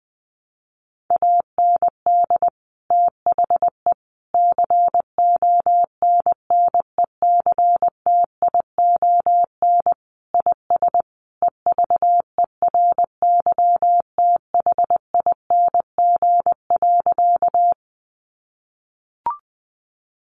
Morse code, heard as a keyed tone, about 20 wpm, 700 Hz. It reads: AND THE CONNECTION IS EVERYTHING.
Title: Sentences from Top 1300 Words 20wpm